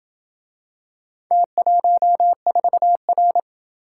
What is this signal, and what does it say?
Morse code, heard as a keyed tone, about 28 wpm, 700 Hz. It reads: T14R